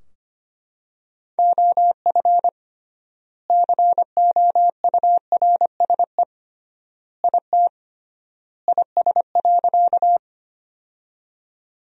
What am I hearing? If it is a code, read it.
Morse code, heard as a keyed tone, about 25 wpm, 700 Hz. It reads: OF COURSE IT IS.
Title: 3x Word-Spacing - Sentences from Top 500 Words - 25wpm